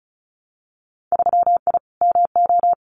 Morse code, heard as a keyed tone, about 34 wpm, 700 Hz. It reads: 3I MO
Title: Winter Field Day Exchanges 35wpm